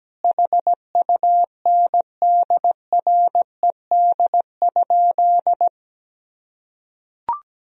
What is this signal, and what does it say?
Morse code, heard as a keyed tone, about 17 wpm, 700 Hz. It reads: HUNDRED?
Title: Sentences from Top 400 Words and Abbreviations 17wpm